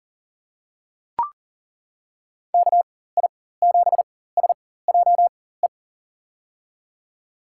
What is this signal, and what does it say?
Morse code, heard as a keyed tone, about 40 wpm, spaced as for 19 wpm, 700 Hz. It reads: KI7SJE
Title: USA Call Signs - 6 Characters - 4x Character Spacing - 40wpm